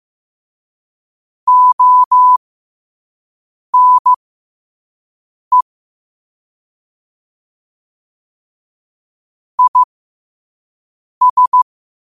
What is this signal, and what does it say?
Morse code, heard as a keyed tone, about 15 wpm, spaced as for 5 wpm, 1000 Hz. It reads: ONE IS
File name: qotdfiles / 5wpm